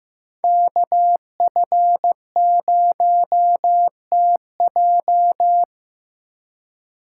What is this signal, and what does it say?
Morse code, heard as a keyed tone, about 15 wpm, 700 Hz. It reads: KF0TJ
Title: USA Call Signs - 5 Characters - 6x Character Spacing - 15wpm